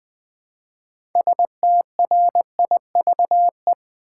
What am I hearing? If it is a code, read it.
Morse code, heard as a keyed tone, about 20 wpm, 700 Hz. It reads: STRIVE
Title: Build A Word - 2 Letters At A Time - 6-Letter Words - 20wpm